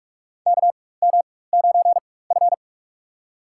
Morse code, heard as a keyed tone, about 46 wpm, spaced as for 21 wpm, 700 Hz. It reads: KM9F
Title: USA Call Signs - 4 Characters - 4x Character Spacing - 45wpm